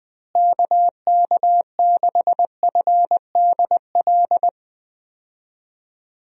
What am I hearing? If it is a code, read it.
Morse code, heard as a keyed tone, about 20 wpm, 700 Hz. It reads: KK6FDL
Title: USA Call Signs - 6 Characters - Repeated 3x - 20wpm